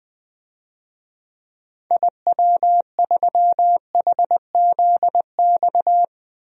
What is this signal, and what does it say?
Morse code, heard as a keyed tone, about 20 wpm, 700 Hz. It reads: IW3HZX
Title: DXCC-100 Call Signs - 6 Characters - 20wpm